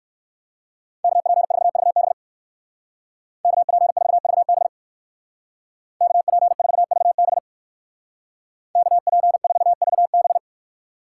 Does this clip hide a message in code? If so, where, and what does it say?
Yes: Morse code from 1.0 to 10.4 s, heard as a keyed tone, about 57 wpm, 700 Hz. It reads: KP4VB KP4VB KP4VB KP4VB